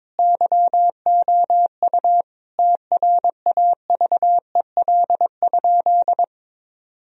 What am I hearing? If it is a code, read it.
Morse code, heard as a keyed tone, about 22 wpm, 700 Hz. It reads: YOU TRAVEL?
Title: Sentences from Top 1000 Words - Review Part 2 - 22wpm